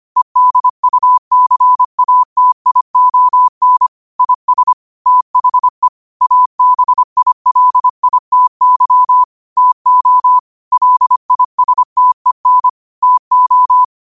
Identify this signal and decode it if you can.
Morse code, heard as a keyed tone, about 25 wpm, 1000 Hz. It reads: EDUCATION IS THE ABILITY TO LISTEN TO
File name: qotdfiles / 25wpm